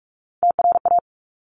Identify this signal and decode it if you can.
Morse code, heard as a keyed tone, about 46 wpm, 690 Hz. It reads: TRA
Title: Top Three Letter Combinations in Words 45wpm